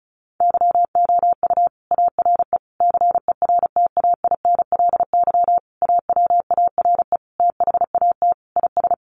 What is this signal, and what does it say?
Morse code, heard as a keyed tone, about 34 wpm, 700 Hz. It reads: YOU ARE CERTAINLY AWARE THAT IS
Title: Sentences from Top 1200 Words 35wpm